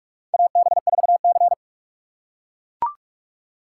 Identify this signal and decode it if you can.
Morse code, heard as a keyed tone, about 44 wpm, 700 Hz. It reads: AB4C
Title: USA Call Signs - 4 Characters - 6x Character Spacing - 45wpm